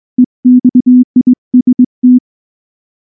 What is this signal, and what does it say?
Morse code, heard as a keyed tone, about 20 wpm, 250 Hz. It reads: EXIST
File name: morse / morse-training-20WPM-10